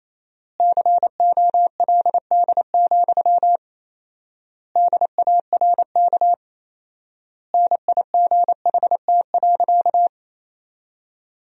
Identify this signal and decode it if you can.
Morse code, heard as a keyed tone, about 27 wpm, 700 Hz. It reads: COLD, DARK NIGHT.
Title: 4x Word-Spacing - Sentences from Top 500 Words - 28wpm